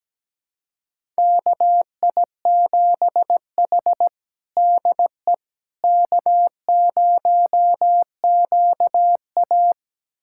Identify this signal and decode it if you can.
Morse code, heard as a keyed tone, about 17 wpm, 700 Hz. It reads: KI7H DE K0QA
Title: USA Callsign DE Callsign - 4 Characters - 17wpm